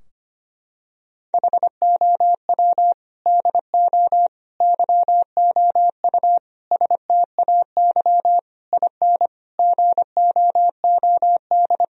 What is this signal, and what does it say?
Morse code, heard as a keyed tone, about 25 wpm, 700 Hz. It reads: HOW DO YOU STAY IN GOOD
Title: Sentences from Top 1000 Words - Review Part 3 - 25wpm